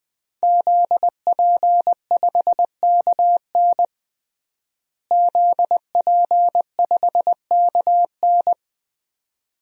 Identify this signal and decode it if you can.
Morse code, heard as a keyed tone, about 21 wpm, 700 Hz. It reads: ZP5KN ZP5KN
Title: DXCC-100 Call Signs - 5 Characters - Repeated 2x - 20wpm